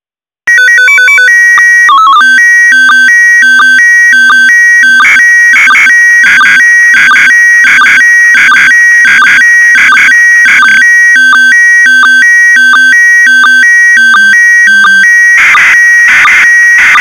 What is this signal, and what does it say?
SSTV (slow-scan television), PD 90.
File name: e9e61cbe-6188-4f30-833c / SSTV OVER FM IMAGE ONE UNDR NET